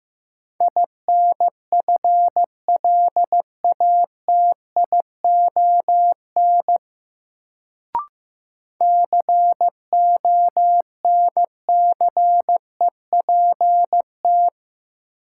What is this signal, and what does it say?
Morse code, heard as a keyed tone, about 15 wpm, 700 Hz. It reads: INFLATION CONCEPT
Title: Sentences from Top 1100 Words - New Word Drill - 15wpm